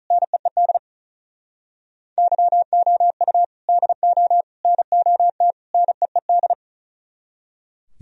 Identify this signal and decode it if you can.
Morse code, heard as a keyed tone, about 46 wpm, 700 Hz. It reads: NEED YOU DO NOT NEED